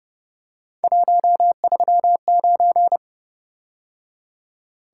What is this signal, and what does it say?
Morse code, heard as a keyed tone, about 29 wpm, 700 Hz. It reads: 139
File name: Speed-Racing - Three Letter-Number Combinations - 25wpm